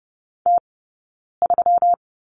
Morse code, heard as a keyed tone, about 30 wpm, 700 Hz. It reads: T 3
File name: Lesson 20d - Number 3 - Two Characters 30wpm